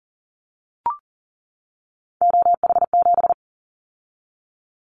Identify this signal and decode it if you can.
Morse code, heard as a keyed tone, about 41 wpm, 700 Hz. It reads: OH7
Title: Callsign Prefixes for Top 50 DXCC - 40wpm